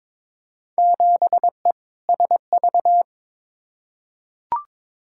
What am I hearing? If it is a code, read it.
Morse code, heard as a keyed tone, about 22 wpm, 700 Hz. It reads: 7E SV
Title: Field Day Exchanges Encore - 22wpm